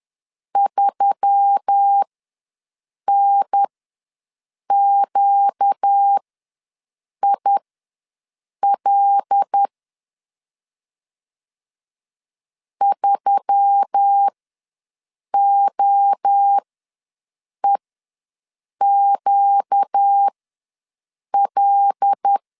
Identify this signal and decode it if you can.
Morse code, heard as a keyed tone, about 10 wpm, spaced as for 6 wpm, 800 Hz. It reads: 3NQIL 3OEQL
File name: LEZIONI / 5 LBVC / 5 L ws 10wpm 3636